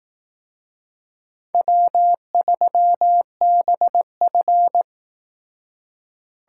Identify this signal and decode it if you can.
Morse code, heard as a keyed tone, about 18 wpm, 700 Hz. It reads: W3BF
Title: Speed Racing - USA 1x2 Call Signs 15wpm